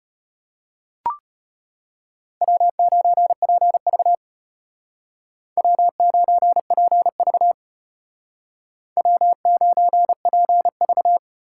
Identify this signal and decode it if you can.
Morse code, heard as a keyed tone, about 32 wpm, 700 Hz. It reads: W9PV W9PV W9PV